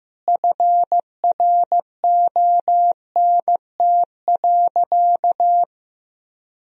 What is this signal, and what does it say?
Morse code, heard as a keyed tone, about 15 wpm, 700 Hz. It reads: FRONT.
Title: Sentences from Top 1000 Words - Review Part 2 - 15wpm